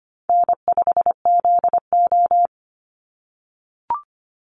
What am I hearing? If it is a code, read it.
Morse code, heard as a keyed tone, about 25 wpm, 700 Hz. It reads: N5ZO
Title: Call Signs - 25wpm